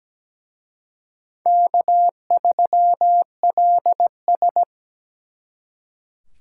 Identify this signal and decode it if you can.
Morse code, heard as a keyed tone, about 20 wpm, 700 Hz. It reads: K3LS
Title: Speed Racing - USA Call Signs 17wpm